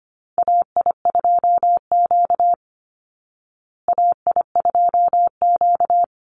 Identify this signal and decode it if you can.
Morse code, heard as a keyed tone, about 25 wpm, 700 Hz. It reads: AI2Q AI2Q
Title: Call Signs Repeated 2x - 25wpm